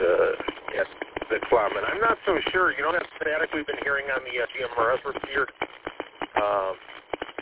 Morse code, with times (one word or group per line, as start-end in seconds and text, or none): none